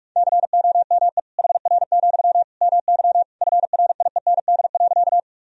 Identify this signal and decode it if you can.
Morse code, heard as a keyed tone, about 41 wpm, 700 Hz. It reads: COME HR, MY FRIEND.
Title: Sentences from Top 300 Words and Abbreviations 45wpm